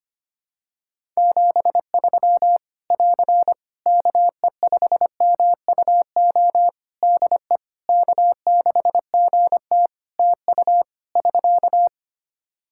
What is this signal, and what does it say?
Morse code, heard as a keyed tone, about 25 wpm, 700 Hz. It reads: 73 <AR> KE5MUO DE K6GT TU <SK>
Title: Practice QSO - Exchange 4 - Ending - 25wpm